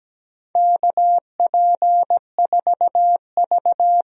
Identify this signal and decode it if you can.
Morse code, heard as a keyed tone, about 17 wpm, 700 Hz. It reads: KP4V
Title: DXCC-100 Call Signs - 4 Characters - 17wpm